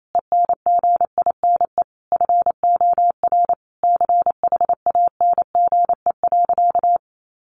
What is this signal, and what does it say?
Morse code, heard as a keyed tone, about 28 wpm, 700 Hz. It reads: ENGINE FOR CHANGE.
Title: Sentences from Top 1500 Words 28wpm